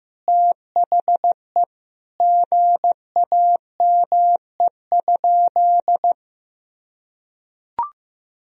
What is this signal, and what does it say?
Morse code, heard as a keyed tone, about 16 wpm, 700 Hz. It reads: THE GAME?
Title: Sentences from Top 1000 Words - Review Part 4 - 15wpm